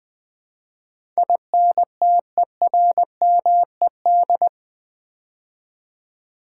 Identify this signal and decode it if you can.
Morse code, heard as a keyed tone, about 20 wpm, 700 Hz. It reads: INTERMED
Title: Build A Word - 4 Letters At A Time - 12-Letter Words - 20wpm